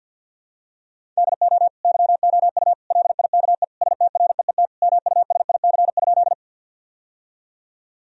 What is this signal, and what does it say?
Morse code, heard as a keyed tone, about 50 wpm, 700 Hz. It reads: DO YOU LIKE STREET MUSIC?
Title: Sentences from Top 1000 Words - Review Part 3 - 50wpm